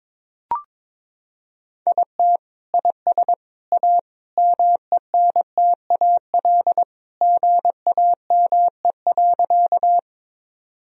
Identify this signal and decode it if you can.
Morse code, heard as a keyed tone, about 22 wpm, 700 Hz. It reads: IT IS A MENTAL GAME.
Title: Sentences from Top 1000 Words - Review Part 4 - 22wpm